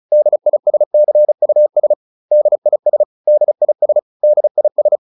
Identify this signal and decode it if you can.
Morse code, heard as a keyed tone, about 34 wpm, 600 Hz. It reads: DISCUS DIS DIS DIS
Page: advanced-practice-files